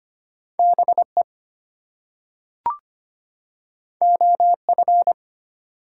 Morse code, heard as a keyed tone, about 25 wpm, 700 Hz. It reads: BE OF
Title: Two Letter Words - 25wpm